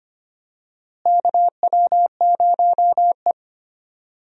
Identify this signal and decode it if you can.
Morse code, heard as a keyed tone, about 25 wpm, 700 Hz. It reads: KW0E
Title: USA Call Signs - 4 Characters - Repeated 3x - 25wpm